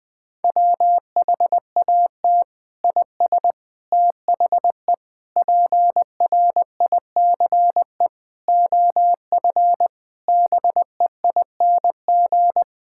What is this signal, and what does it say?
Morse code, heard as a keyed tone, about 20 wpm, 700 Hz. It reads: WHAT IS THE PRICE OF BEING
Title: Sentences from Top 1000 Words - Review Part 5 - 20wpm